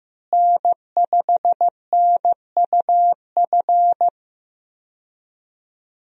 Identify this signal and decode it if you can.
Morse code, heard as a keyed tone, about 15 wpm, 700 Hz. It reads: N5NUF
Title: USA Call Signs - 5 Characters - Repeated 3x - 15wpm